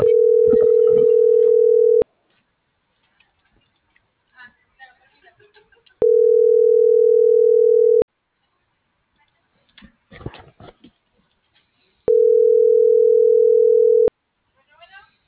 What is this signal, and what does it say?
Ringback tone (North America), 440 + 480 Hz.